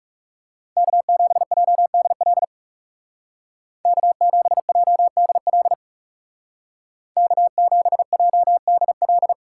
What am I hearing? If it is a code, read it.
Morse code, heard as a keyed tone, about 39 wpm, 700 Hz. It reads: K7JDL K7JDL K7JDL